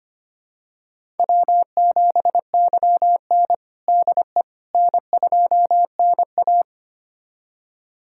Morse code, heard as a keyed tone, about 26 wpm, 700 Hz. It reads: W7YN DE N2NA
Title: USA Callsign DE Callsign - 4 Characters - 25wpm